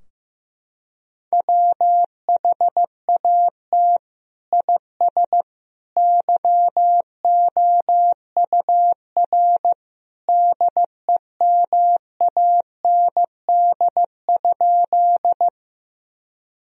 Morse code, heard as a keyed tone, about 15 wpm, 700 Hz. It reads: WHAT IS YOUR DEMAND?
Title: Sentences from Top 1000 Words - Review Part 4 - 15wpm